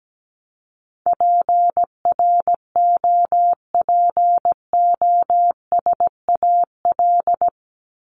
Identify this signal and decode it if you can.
Morse code, heard as a keyed tone, about 17 wpm, 700 Hz. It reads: PROPOSAL
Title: Top 1751-2000 Words 17wpm